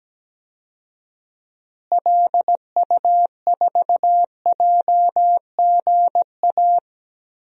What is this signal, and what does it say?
Morse code, heard as a keyed tone, about 17 wpm, 700 Hz. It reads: LU4JGA
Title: DXCC-100 Call Signs - 6 Characters - Repeated 3x - 17wpm